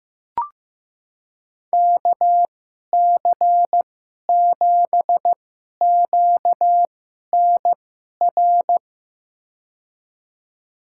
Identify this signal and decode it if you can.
Morse code, heard as a keyed tone, about 15 wpm, spaced as for 11 wpm, 700 Hz. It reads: KC7QNR
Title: USA Call Signs - 2x Character Spacing - 15wpm